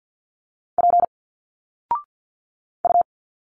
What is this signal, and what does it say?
Morse code, heard as a keyed tone, about 50 wpm, 700 Hz. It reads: P U
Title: Single Letters 50wpm